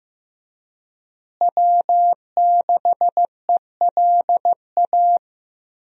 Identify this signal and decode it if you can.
Morse code, heard as a keyed tone, about 15 wpm, 700 Hz. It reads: W6ELA
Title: USA Call Signs - 5 Characters - Repeated 3x - 15wpm